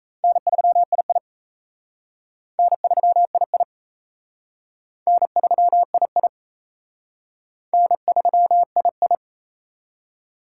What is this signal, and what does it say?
Morse code, heard as a keyed tone, about 34 wpm, 700 Hz. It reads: N3II N3II N3II N3II
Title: Speed Racing - USA 1x2 Call Signs 28wpm